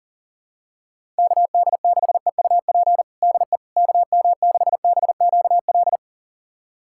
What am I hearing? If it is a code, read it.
Morse code, heard as a keyed tone, about 40 wpm, 700 Hz. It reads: KD6EUP DE KM6BQL
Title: USA Callsign DE Callsign - 6 Characters - 40wpm